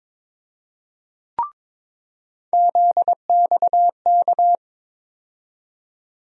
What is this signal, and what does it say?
Morse code, heard as a keyed tone, about 22 wpm, 700 Hz. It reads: ZXK
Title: USA Call Sign Postfixes - 3 Letters 22wpm